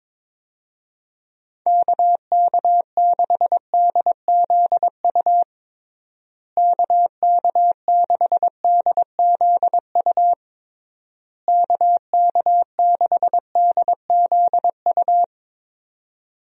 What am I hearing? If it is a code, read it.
Morse code, heard as a keyed tone, about 22 wpm, 700 Hz. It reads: KK6DZU KK6DZU KK6DZU
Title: USA Call Signs - 6 Characters - Repeated 3x - 22wpm